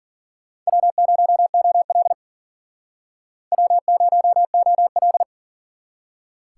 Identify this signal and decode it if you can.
Morse code, heard as a keyed tone, about 52 wpm, 700 Hz. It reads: W0OL W0OL